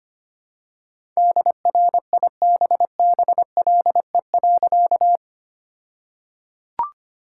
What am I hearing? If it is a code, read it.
Morse code, heard as a keyed tone, about 25 wpm, 700 Hz. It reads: DRIBBLE.
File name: practice / 25wpm / All-Age Appropriate Jokes - 25wpm